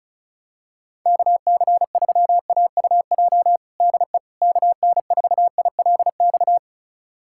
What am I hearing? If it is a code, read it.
Morse code, heard as a keyed tone, about 36 wpm, 700 Hz. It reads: KC3AUJ DE KN4ILX
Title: USA Callsign DE Callsign - 6 Characters - 35wpm